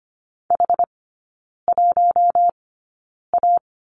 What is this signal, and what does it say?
Morse code, heard as a keyed tone, about 25 wpm, spaced as for 9 wpm, 700 Hz. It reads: H1A
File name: course / 25:10wpm / 093 Lesson 16e - Letter c - Three Characters 25wpm